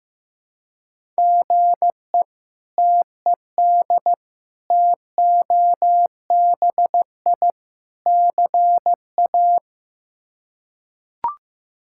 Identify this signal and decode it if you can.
Morse code, heard as a keyed tone, about 15 wpm, 700 Hz. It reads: GE TED TOBI CA
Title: K1USN SST QSO Practice - 15wpm